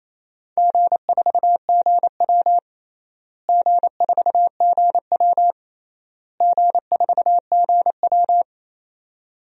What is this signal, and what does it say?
Morse code, heard as a keyed tone, about 27 wpm, 700 Hz. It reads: G4GW G4GW G4GW
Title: DXCC-100 Call Signs - 4 Characters - Repeated 3x - 28wpm